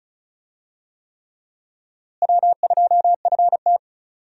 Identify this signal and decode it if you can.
Morse code, heard as a keyed tone, about 36 wpm, 700 Hz. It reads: W2FT
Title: USA Call Signs - 4 Characters - Repeated 3x - 35wpm